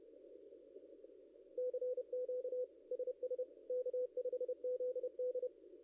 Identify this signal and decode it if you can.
Morse code, heard as a keyed tone, about 31 wpm, 500 Hz. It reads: CQ SS K5ZD
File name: contests / ss-cw-k5tr-op-n5rz -0600